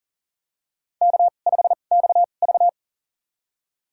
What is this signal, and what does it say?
Morse code, heard as a keyed tone, about 40 wpm, spaced as for 29 wpm, 700 Hz. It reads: K5XV
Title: USA Call Signs - 4 Characters - 2x Character Spacing - 40wpm